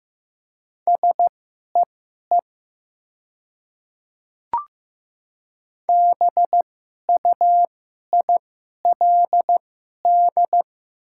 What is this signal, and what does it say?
Morse code, heard as a keyed tone, about 15 wpm, spaced as for 11 wpm, 700 Hz. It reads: SEE BUILD
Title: 2x Character Spacing - Top 200 Words - 15wpm